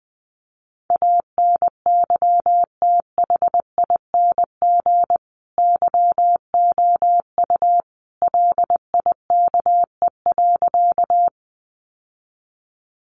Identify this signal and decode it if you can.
Morse code, heard as a keyed tone, about 20 wpm, 700 Hz. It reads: ANYTHING YOU LIKE.
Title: Sentences from Top 1000 Words 20wpm